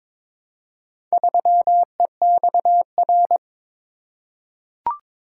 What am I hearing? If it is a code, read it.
Morse code, heard as a keyed tone, about 22 wpm, 700 Hz. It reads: 3EXR
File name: practice / 22wpm / Four Letter-Number Combinations - 22wpm